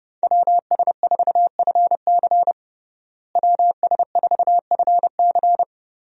Morse code, heard as a keyed tone, about 32 wpm, 700 Hz. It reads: WS4FC WS4FC
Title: USA Call Signs - 5 Characters - Repeated 3x - 30wpm